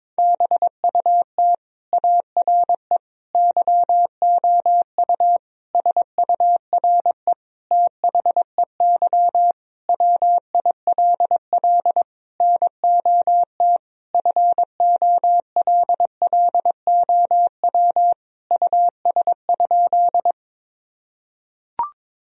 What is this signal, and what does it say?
Morse code, heard as a keyed tone, about 22 wpm, 700 Hz. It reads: BUT ARE YOU SURE THEY WILL NOT FOLLOW US?
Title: Sentences from Top 1000 Words - Review Part 1 - 22wpm